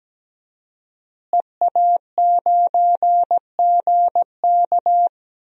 Morse code, heard as a keyed tone, about 17 wpm, 700 Hz. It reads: EA9GK
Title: DXCC-100 Call Signs - 5 Characters - Repeated 2x - 17wpm